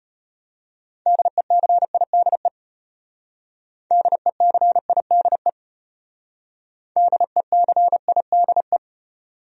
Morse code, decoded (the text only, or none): DECIDE DECIDE DECIDE